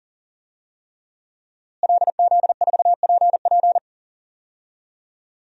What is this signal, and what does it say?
Morse code, heard as a keyed tone, about 40 wpm, 700 Hz. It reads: LZ4PP
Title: DXCC-100 Call Signs - 5 Characters - Repeated 3x - 40wpm